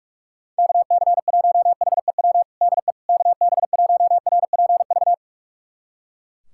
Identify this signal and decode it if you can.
Morse code, heard as a keyed tone, about 43 wpm, 700 Hz. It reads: KC1HEW DE KB1RPV
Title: USA Callsign DE Callsign - 6 Characters - 45wpm